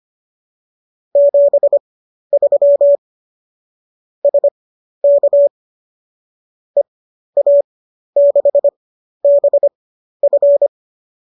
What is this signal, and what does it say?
Morse code, heard as a keyed tone, about 25 wpm, spaced as for 12 wpm, 570 Hz. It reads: 73 SK EA6BF
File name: qso009 12